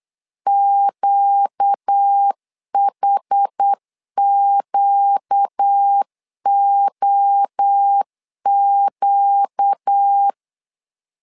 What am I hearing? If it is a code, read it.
Morse code, heard as a keyed tone, about 8 wpm, 800 Hz. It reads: QHQOQ
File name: LEZIONI / 4 Q 8wpm 2732